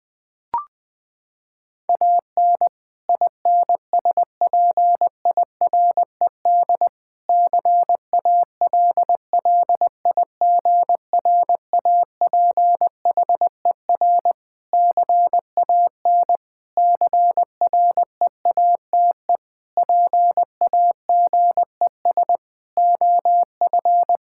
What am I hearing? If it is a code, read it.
Morse code, heard as a keyed tone, about 20 wpm, 700 Hz. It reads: AN INSPIRED CALLIGRAPHER CAN CREATE PAGES OF
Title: Pangram Challenge v2 - 20wpm